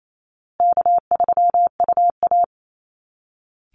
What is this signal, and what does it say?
Morse code, heard as a keyed tone, about 28 wpm, 700 Hz. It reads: K3UA
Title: Call Signs Repeated 2x - 28wpm